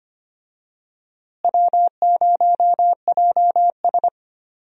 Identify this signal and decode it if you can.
Morse code, heard as a keyed tone, about 25 wpm, 700 Hz. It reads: W0JS
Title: USA Call Signs - 4 Characters - Repeated 3x - 25wpm